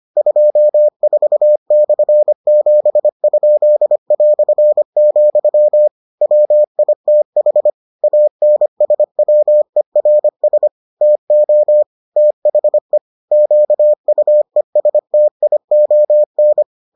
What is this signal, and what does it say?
Morse code, heard as a keyed tone, about 25 wpm, 600 Hz. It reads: 24/7?", WITH ANSWERS TO THE QUESTION